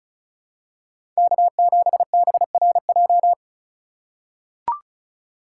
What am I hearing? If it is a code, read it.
Morse code, heard as a keyed tone, about 36 wpm, 700 Hz. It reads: K7BRJ
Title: USA Call Signs - 2x Character Spacing - 35wpm